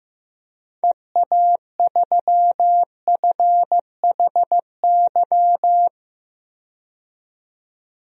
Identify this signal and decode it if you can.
Morse code, heard as a keyed tone, about 15 wpm, 700 Hz. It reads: EA3FHY
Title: DXCC-100 Call Signs - 6 Characters - 15wpm